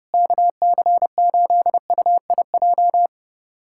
Morse code, heard as a keyed tone, about 32 wpm, 700 Hz. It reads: KC8UIJ
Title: USA Callsign DE Callsign - 6 Characters - 30wpm